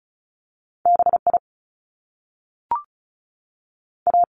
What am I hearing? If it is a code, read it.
Morse code, heard as a keyed tone, about 36 wpm, spaced as for 6 wpm, 700 Hz. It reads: BIA
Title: Top Two Letter Combinations in Words 35wpm